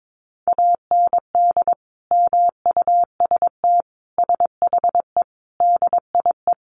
Morse code, heard as a keyed tone, about 22 wpm, 700 Hz. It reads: AND MUST SHE DIE
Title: Sentences from Top 1300 Words 22wpm